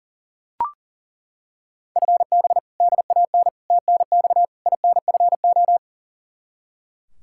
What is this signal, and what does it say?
Morse code, heard as a keyed tone, about 37 wpm, 700 Hz. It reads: FB DAN TNX INFO
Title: Practice QSO - Exchange 3 - More Info - 40wpm